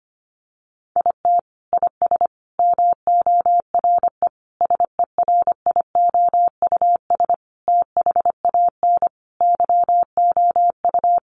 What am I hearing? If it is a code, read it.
Morse code, heard as a keyed tone, about 25 wpm, 700 Hz. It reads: IT IS MORE SERIOUS THAN YOU
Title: Sentences from Top 1100 Words 25wpm